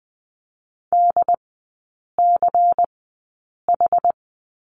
Morse code, heard as a keyed tone, about 20 wpm, spaced as for 8 wpm, 700 Hz. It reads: DCH
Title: 093 Lesson 16e - Letter c - Three Characters 20wpm